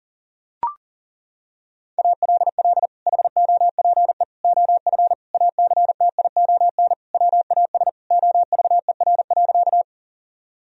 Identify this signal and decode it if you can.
Morse code, heard as a keyed tone, about 40 wpm, 700 Hz. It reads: ALL HOPE OF ACTION WAS OVER.
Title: Sentences from Top 1000 Words - Review Part 4 - 40wpm